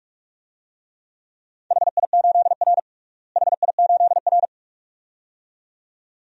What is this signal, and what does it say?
Morse code, heard as a keyed tone, about 46 wpm, 700 Hz. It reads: HI8R HI8R
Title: DXCC-100 Call Signs - 4 Characters - Repeated 2x - 45wpm